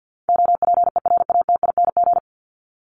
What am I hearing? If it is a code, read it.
Morse code, heard as a keyed tone, about 49 wpm, 700 Hz. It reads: OPERATING
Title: ING Suffix - Top 100 Words with Base Words 50wpm